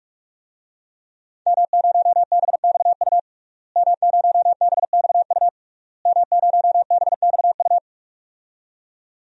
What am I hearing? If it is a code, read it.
Morse code, heard as a keyed tone, about 46 wpm, 700 Hz. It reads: M0BXU M0BXU M0BXU
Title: DXCC-100 Call Signs - 5 Characters - Repeated 3x - 45wpm